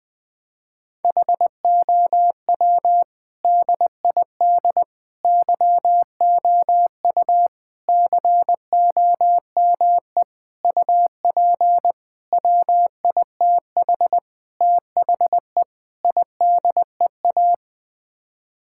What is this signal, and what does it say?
Morse code, heard as a keyed tone, about 20 wpm, 700 Hz. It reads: HOW DID YOU COME UP WITH THE IDEA
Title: Sets of 8 Words - 20wpm